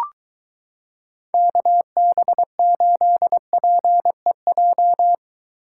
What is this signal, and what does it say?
Morse code, heard as a keyed tone, about 23 wpm, 700 Hz. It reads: KB8PEJ